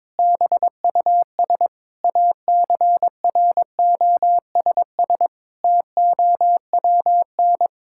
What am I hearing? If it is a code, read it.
Morse code, heard as a keyed tone, about 21 wpm, 700 Hz. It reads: BUS ACROSS TOWN
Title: Sentences from Top 1000 Words - Review Part 6 - 22wpm